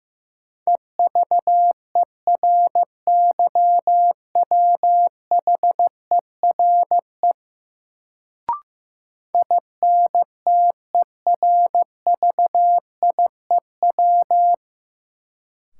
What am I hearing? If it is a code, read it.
Morse code, heard as a keyed tone, about 15 wpm, 700 Hz. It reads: EVERYWHERE INTERVIEW
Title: Sentences from Top 1600 Words - New Word Drill - 15wpm